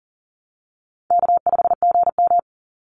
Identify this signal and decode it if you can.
Morse code, heard as a keyed tone, about 38 wpm, 700 Hz. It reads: K5GM
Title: Easy Call Signs Repeated 2x - 40wpm